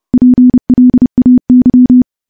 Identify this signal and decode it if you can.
Morse code, heard as a keyed tone, about 30 wpm, 250 Hz. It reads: PLAY
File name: morse / morse-training-30WPM